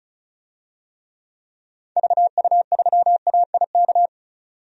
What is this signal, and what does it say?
Morse code, heard as a keyed tone, about 34 wpm, 700 Hz. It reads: VU3AIK
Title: DXCC-100 Call Signs - 6 Characters - 35wpm